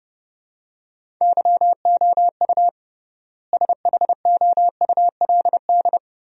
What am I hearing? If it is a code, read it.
Morse code, heard as a keyed tone, about 32 wpm, 700 Hz. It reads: YOU SHOULD
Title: 3x Word-Spacing - Sentences from Top 400 Words - 30wpm